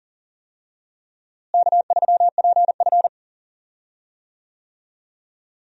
Morse code, heard as a keyed tone, about 37 wpm, 700 Hz. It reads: K3PF
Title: USA Call Signs - 4 Characters - Repeated 2x - 40wpm